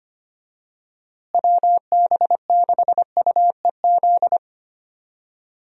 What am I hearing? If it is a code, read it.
Morse code, heard as a keyed tone, about 25 wpm, 700 Hz. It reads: WB6UEZ
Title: USA Call Signs - 6 Characters - Repeated 2x - 25wpm